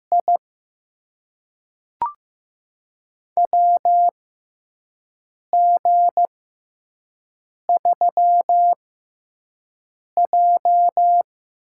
Morse code, heard as a keyed tone, about 15 wpm, spaced as for 5 wpm, 700 Hz. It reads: I WG3J
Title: USA Call Signs - 4 Characters - 6x Character Spacing - 15wpm